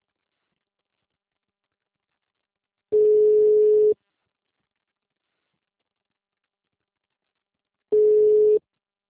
Ringback tone (Europe, 425 Hz).
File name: out- -2024- - - .623